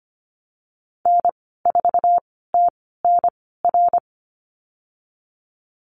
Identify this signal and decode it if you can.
Morse code, heard as a keyed tone, about 25 wpm, spaced as for 16 wpm, 700 Hz. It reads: N4TNR
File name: Lesson 15g - Number 5 - Call Signs 25wpm